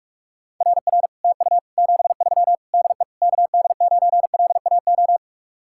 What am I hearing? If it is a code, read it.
Morse code, heard as a keyed tone, about 48 wpm, 700 Hz. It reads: RR TU 73 DE KD9LAO